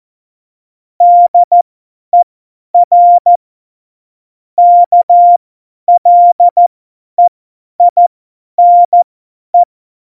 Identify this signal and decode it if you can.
Morse code, heard as a keyed tone, about 14 wpm, spaced as for 10 wpm, 700 Hz. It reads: DER KLEINE
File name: Morsekurs / Witze / Witz 12. Dezember 2017, 50 F70